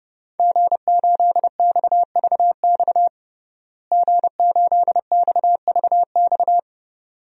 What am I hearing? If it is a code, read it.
Morse code, heard as a keyed tone, about 32 wpm, 700 Hz. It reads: G8XVX G8XVX
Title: DXCC-100 Call Signs - 5 Characters - Repeated 2x - 30wpm